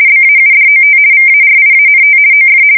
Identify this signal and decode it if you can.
RTTY, about 50 baud. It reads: BAUDOT VARIANT OF
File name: fsk170-50B / teletype-corp